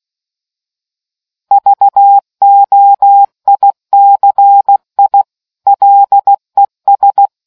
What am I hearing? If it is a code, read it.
Morse code, heard as a keyed tone, about 16 wpm, 800 Hz. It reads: VOICI LES